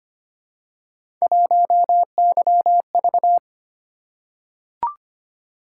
Morse code, heard as a keyed tone, about 25 wpm, 700 Hz. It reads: 1YV
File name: Three Letter-Number Combinations - 25wpm